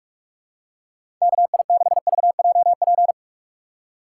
Morse code, heard as a keyed tone, about 44 wpm, 690 Hz. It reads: KI6VJP
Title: USA Call Signs - 4x Character Spacing - 45wpm